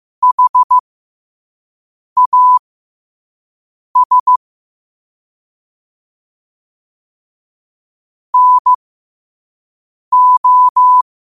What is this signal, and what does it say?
Morse code, heard as a keyed tone, about 15 wpm, spaced as for 5 wpm, 1000 Hz. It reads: HAS NO
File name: qotdfiles / 5wpm